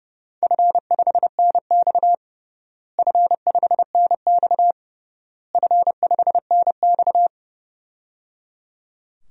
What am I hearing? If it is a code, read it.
Morse code, heard as a keyed tone, about 30 wpm, 700 Hz. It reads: F5NX F5NX F5NX
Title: DXCC-100 Call Signs - 4 Characters - Repeated 3x - 30wpm